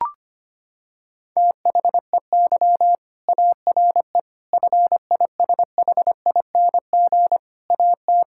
Morse code, heard as a keyed tone, about 25 wpm, 700 Hz. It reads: THEY ARE FISHING AT
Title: Sentences from Top 1000 Words - Review Part 5 - 25wpm